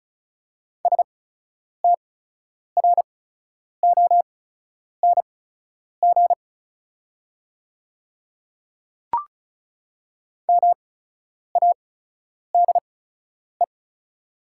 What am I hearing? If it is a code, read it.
Morse code, heard as a keyed tone, about 36 wpm, spaced as for 10 wpm, 700 Hz. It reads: STRONG MADE
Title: 8x Character Spacing - Top 500 Words - 35wpm